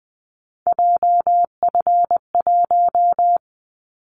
Morse code, heard as a keyed tone, about 19 wpm, 700 Hz. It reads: JF1
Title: Callsign Prefixes for Top 25 DXCC - 20wpm